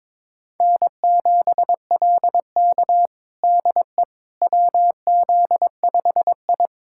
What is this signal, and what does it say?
Morse code, heard as a keyed tone, about 22 wpm, 700 Hz. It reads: N7LK DE WZ5I
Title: USA Callsign DE Callsign - 4 Characters - 22wpm